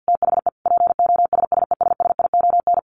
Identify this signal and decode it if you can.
Morse code, heard as a keyed tone, about 50 wpm, 700 Hz. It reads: THE POSSESSION
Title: Sentences from Top 1400 Words 50wpm